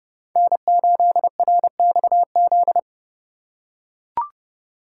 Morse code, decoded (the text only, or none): N8RXZ